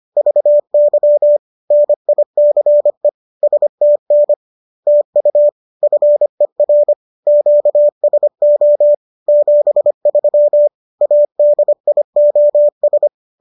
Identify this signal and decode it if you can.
Morse code, heard as a keyed tone, about 25 wpm, 590 Hz. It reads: VY NICE STN TU FER QSO 73 ADIOS
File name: qso204 25